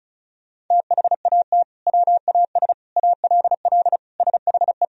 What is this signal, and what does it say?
Morse code, heard as a keyed tone, about 36 wpm, 700 Hz. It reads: THAT WAS ALL SHE
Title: Sentences from Top 100 Words and Abbreviations 35wpm